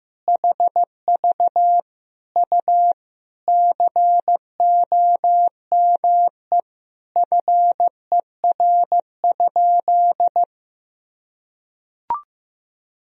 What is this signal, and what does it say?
Morse code, heard as a keyed tone, about 15 wpm, 700 Hz. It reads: HV U COME FER?
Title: Sentences from Top 100 Words and Abbreviations 15wpm